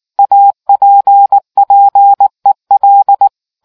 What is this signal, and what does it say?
Morse code, heard as a keyed tone, about 19 wpm, 800 Hz. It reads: APPEL